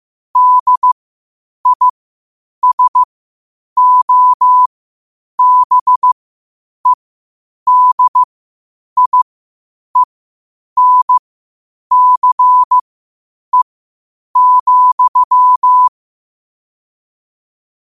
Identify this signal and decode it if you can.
Morse code, heard as a keyed tone, about 15 wpm, spaced as for 8 wpm, 1000 Hz. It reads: DISOBEDIENCE,
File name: qotdfiles / 8wpm